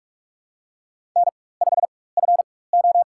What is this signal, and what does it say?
Morse code, heard as a keyed tone, about 42 wpm, spaced as for 21 wpm, 700 Hz. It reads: N5FO
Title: USA Call Signs - 4x Character Spacing - 45wpm